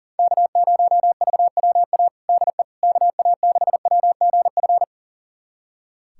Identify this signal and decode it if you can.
Morse code, heard as a keyed tone, about 40 wpm, 700 Hz. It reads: K0VWA DE KA6WGF
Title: USA Callsign DE Callsign 40wpm